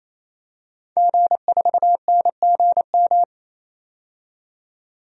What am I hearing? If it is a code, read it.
Morse code, heard as a keyed tone, about 28 wpm, 700 Hz. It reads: G4NGM
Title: DXCC-100 Call Signs - 5 Characters - 28wpm